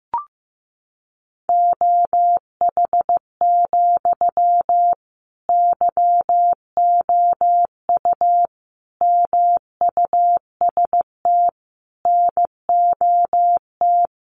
Morse code, heard as a keyed tone, about 15 wpm, 700 Hz. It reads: OH, YOU MUST NOT
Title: Sentences from Top 1100 Words 15wpm